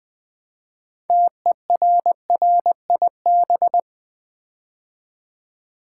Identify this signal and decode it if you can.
Morse code, heard as a keyed tone, about 21 wpm, 700 Hz. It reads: TERRIB
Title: Build A Word - 2 Letters At A Time - 8-Letter Words - 20wpm